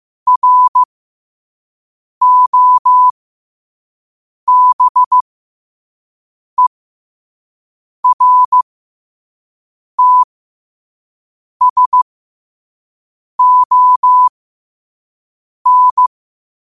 Morse code, heard as a keyed tone, about 15 wpm, spaced as for 5 wpm, 1000 Hz. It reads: ROBERTSON
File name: qotdfiles / 5wpm